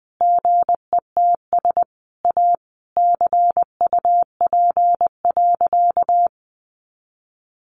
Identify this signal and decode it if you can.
Morse code, heard as a keyed tone, about 20 wpm, 700 Hz. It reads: GETS A CUP.
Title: Easy Phrases 20wpm